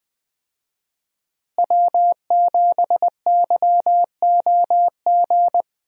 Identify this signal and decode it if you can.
Morse code, heard as a keyed tone, about 20 wpm, 700 Hz. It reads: W7YOG
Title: USA Call Signs - 5 Characters - Repeated 3x - 20wpm